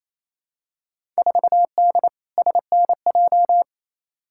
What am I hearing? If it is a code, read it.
Morse code, heard as a keyed tone, about 28 wpm, 700 Hz. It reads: 4D SNJ
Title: Field Day Exchanges Encore - 28wpm